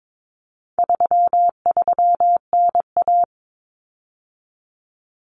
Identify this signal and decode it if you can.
Morse code, heard as a keyed tone, about 22 wpm, 700 Hz. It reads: 33NA
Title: Common Cut Numbers - 4-Digit 22wpm